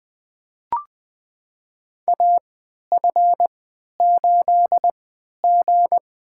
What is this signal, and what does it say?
Morse code, heard as a keyed tone, about 20 wpm, spaced as for 11 wpm, 700 Hz. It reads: AF8G
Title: USA Call Signs - 4 Characters - 3x Character Spacing - 20wpm